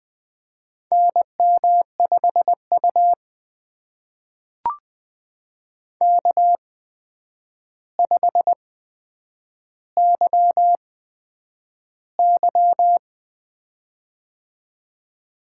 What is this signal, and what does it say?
Morse code, heard as a keyed tone, about 20 wpm, spaced as for 9 wpm, 700 Hz. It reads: NM5U K5YY
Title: USA Call Signs - 4 Characters - 8x Character Spacing - 20wpm